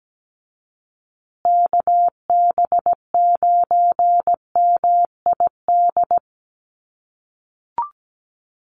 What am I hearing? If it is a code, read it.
Morse code, heard as a keyed tone, about 17 wpm, 700 Hz. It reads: KB9MID
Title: USA Call Signs Repeated 2x 17wpm